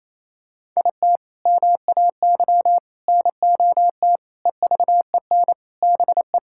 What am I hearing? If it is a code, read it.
Morse code, heard as a keyed tone, about 28 wpm, 700 Hz. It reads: IT MAY NOT EVEN BE
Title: Sentences from Top 1000 Words - Review Part 6 - 28wpm